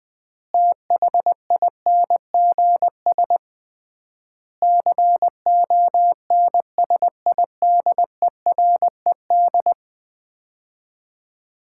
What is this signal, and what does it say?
Morse code, heard as a keyed tone, about 20 wpm, 700 Hz. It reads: THINGS CONSIDERED
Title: 3x Word-Spacing - Sets of 5 Words - 20wpm